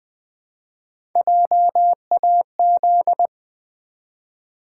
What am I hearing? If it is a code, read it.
Morse code, heard as a keyed tone, about 20 wpm, 700 Hz. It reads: JAZ
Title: Build A Word - 1 Letter At A Time - 4-Letter Words - 20wpm